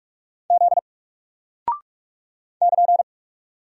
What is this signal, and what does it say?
Morse code, heard as a keyed tone, about 44 wpm, 700 Hz. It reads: Z <KN>
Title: Single Character (Letter:Number:Punctuation:Prosign) Alt 45wpm